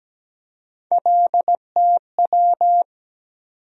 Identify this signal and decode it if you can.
Morse code, heard as a keyed tone, about 17 wpm, 700 Hz. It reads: LTW
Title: USA Call Sign Postfixes - 3 Letters 17wpm